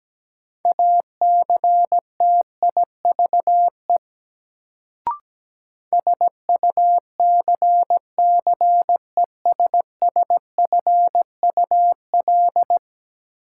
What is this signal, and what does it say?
Morse code, heard as a keyed tone, about 17 wpm, 700 Hz. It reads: ACTIVE SUCCESSFUL
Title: Sentences from Top 900 Words - New Word Drill - 17wpm